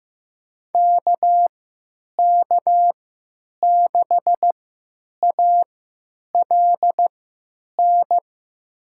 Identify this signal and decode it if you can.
Morse code, heard as a keyed tone, about 15 wpm, spaced as for 9 wpm, 700 Hz. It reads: KK6ALN
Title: USA Call Signs - 6 Characters - 3x Character Spacing - 15wpm